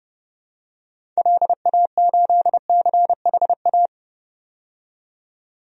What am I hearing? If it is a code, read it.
Morse code, heard as a keyed tone, about 32 wpm, 700 Hz. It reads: LA8CHA
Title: DXCC-100 Call Signs - 6 Characters - 30wpm